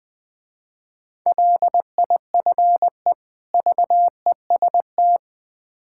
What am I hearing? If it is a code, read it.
Morse code, heard as a keyed tone, about 23 wpm, 700 Hz. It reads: LIFE VEST
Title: Difficult Words - Speed-Racing with Hints - 20wpm